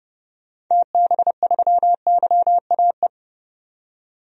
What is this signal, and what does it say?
Morse code, heard as a keyed tone, about 30 wpm, 700 Hz. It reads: TB3YAE
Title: DXCC-100 Call Signs - 6 Characters - 30wpm